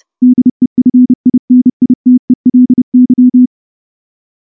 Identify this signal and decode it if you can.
Morse code, heard as a keyed tone, about 28 wpm, 250 Hz. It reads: DEFINITELY
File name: morse-training-30WPM